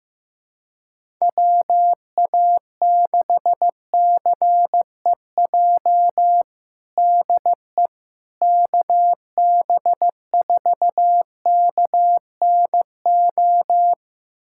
Morse code, heard as a keyed tone, about 15 wpm, 700 Hz. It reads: WA6CEJ DE KB4KNO
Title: USA Callsign DE Callsign - 6 Characters - 15wpm